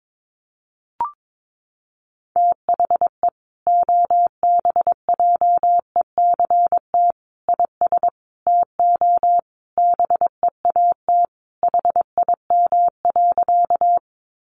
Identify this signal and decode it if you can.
Morse code, heard as a keyed tone, about 22 wpm, 700 Hz. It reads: THE OBJECT IS TO BEAT HIM.
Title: Sentences from Top 1300 Words 22wpm